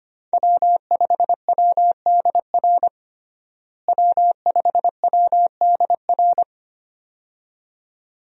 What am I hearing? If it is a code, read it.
Morse code, heard as a keyed tone, about 25 wpm, 700 Hz. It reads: W5WDR W5WDR
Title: USA Call Signs - 5 Characters - Repeated 2x - 25wpm